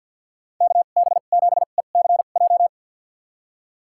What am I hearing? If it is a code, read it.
Morse code, heard as a keyed tone, about 52 wpm, spaced as for 36 wpm, 700 Hz. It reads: KB7ECJ
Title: USA Call Signs - 2x Character Spacing - 50wpm